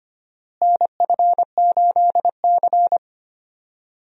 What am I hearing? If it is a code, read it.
Morse code, heard as a keyed tone, about 25 wpm, 700 Hz. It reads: NF8C
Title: USA Call Signs - 4 Characters - Repeated 3x - 25wpm